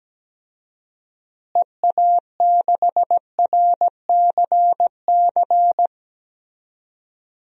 Morse code, heard as a keyed tone, about 17 wpm, 700 Hz. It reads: EA6RCC
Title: DXCC-100 Call Signs - 6 Characters - Repeated 3x - 17wpm